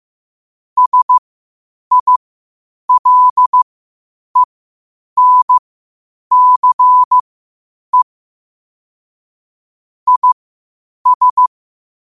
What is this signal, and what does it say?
Morse code, heard as a keyed tone, about 15 wpm, spaced as for 8 wpm, 1000 Hz. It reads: SILENCE IS
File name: qotdfiles / 8wpm